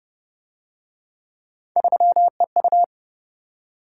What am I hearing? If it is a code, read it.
Morse code, heard as a keyed tone, about 33 wpm, 700 Hz. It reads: 3EU